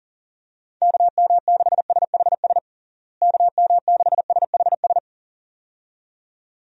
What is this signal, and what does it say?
Morse code, heard as a keyed tone, about 37 wpm, 700 Hz. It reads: KM6SHS KM6SHS
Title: USA Call Signs - 6 Characters - Repeated 2x - 40wpm